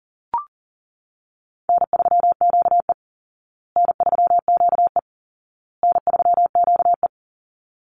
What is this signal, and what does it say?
Morse code, heard as a keyed tone, about 40 wpm, 700 Hz. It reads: N3QE N3QE N3QE
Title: Easy Call Signs Repeated 3x - 40wpm